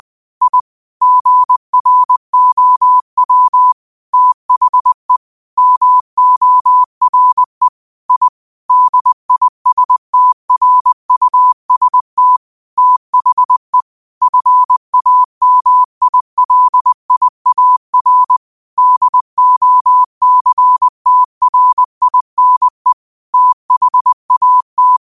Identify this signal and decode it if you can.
Morse code, heard as a keyed tone, about 20 wpm, 1000 Hz. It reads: I GROW THE MORE I DISTRUST THE FAMILIAR DOCTRINE THAT
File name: qotdfiles / 20wpm